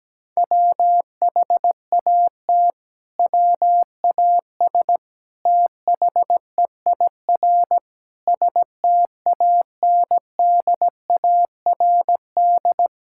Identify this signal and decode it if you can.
Morse code, heard as a keyed tone, about 17 wpm, 700 Hz. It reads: WHAT WAS THEIR STANDARD
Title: Sentences from Top 1000 Words - Review Part 6 - 17wpm